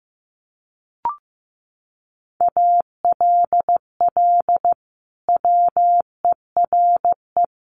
Morse code, heard as a keyed tone, about 15 wpm, 700 Hz. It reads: ALL WERE
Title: Sentences from Top 1400 Words 15wpm